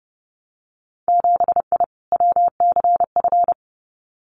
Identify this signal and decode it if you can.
Morse code, heard as a keyed tone, about 30 wpm, 700 Hz. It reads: 7I WCF
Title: Winter Field Day Exchanges Encore 30wpm